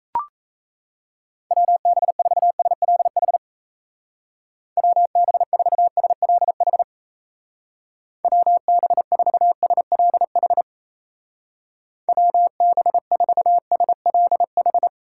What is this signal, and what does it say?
Morse code, heard as a keyed tone, about 34 wpm, 700 Hz. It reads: WB4SLH WB4SLH WB4SLH WB4SLH
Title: Speed Racing - USA 2x3 Call Signs 28wpm